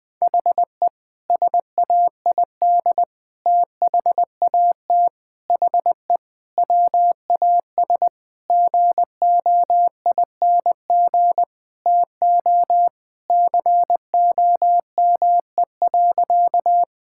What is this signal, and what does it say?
Morse code, heard as a keyed tone, about 20 wpm, 700 Hz. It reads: HE SAID THAT HE WAS GOING TO COME.
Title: Sentences with 5-Word Phrases - 20wpm